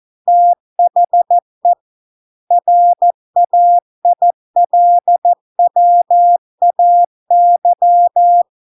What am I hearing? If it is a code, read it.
Morse code, heard as a keyed tone, about 14 wpm, 700 Hz. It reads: THE RAILWAY